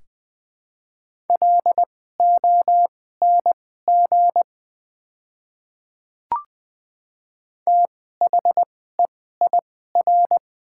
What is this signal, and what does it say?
Morse code, heard as a keyed tone, about 20 wpm, spaced as for 15 wpm, 700 Hz. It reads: LONG THEIR
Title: 2x Character Spacing - Top 100 Words - 20wpm